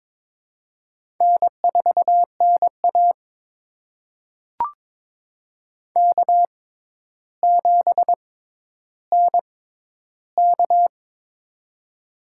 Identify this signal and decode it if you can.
Morse code, heard as a keyed tone, about 22 wpm, spaced as for 11 wpm, 700 Hz. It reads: N4NA K7NK
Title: USA Call Signs - 4 Characters - 6x Character Spacing - 22wpm